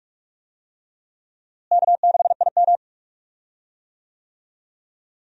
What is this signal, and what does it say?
Morse code, heard as a keyed tone, about 46 wpm, 700 Hz. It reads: K6IM
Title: USA Call Signs - 4 Characters - Repeated 3x - 45wpm